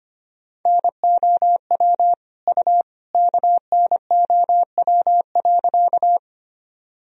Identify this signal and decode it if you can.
Morse code, heard as a keyed tone, about 25 wpm, 700 Hz. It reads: NOW U KNOW.
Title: Sentences from Top 100 Words and Abbreviations 25wpm